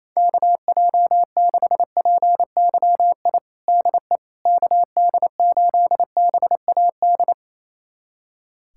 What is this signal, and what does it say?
Morse code, heard as a keyed tone, about 28 wpm, 700 Hz. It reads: KJ6PYI DE KD8BAD
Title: USA Callsign DE Callsign - 6 Characters - 28wpm